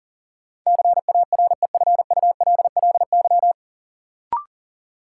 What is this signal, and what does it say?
Morse code, heard as a keyed tone, about 40 wpm, 700 Hz. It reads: CAREFULLY
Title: Sentences from Top 1300 Words - New Word Drill Alt - 40wpm